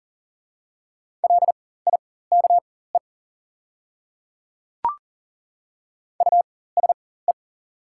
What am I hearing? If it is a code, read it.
Morse code, heard as a keyed tone, about 39 wpm, spaced as for 19 wpm, 690 Hz. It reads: LIKE USE
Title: 4x Character Spacing - Top 200 Words - 40wpm